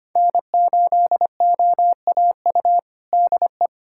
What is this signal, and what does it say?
Morse code, heard as a keyed tone, about 25 wpm, 700 Hz. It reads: N8OAU DE
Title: USA Callsign DE Callsign - 5 Characters - 25wpm